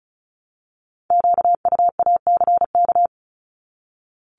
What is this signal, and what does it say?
Morse code, heard as a keyed tone, about 36 wpm, 700 Hz. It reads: QUACK
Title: Multi-letter Phonemes Repeated 3x 35wpm